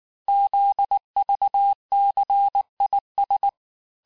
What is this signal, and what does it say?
Morse code, heard as a keyed tone, about 19 wpm, 800 Hz. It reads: ZVCIS
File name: cours-du-lundi files / cours-1595-teste-CHIFFRE-vitesse18-MPM